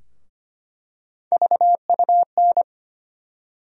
Morse code, heard as a keyed tone, about 25 wpm, 700 Hz. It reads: VUN
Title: USA Call Sign Postfixes - 3 Letters 25wpm